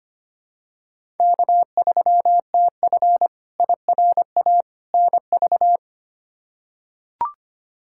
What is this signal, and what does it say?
Morse code, heard as a keyed tone, about 25 wpm, 700 Hz. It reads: K3TF IRA NV
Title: K1USN SST QSO Practice - 25wpm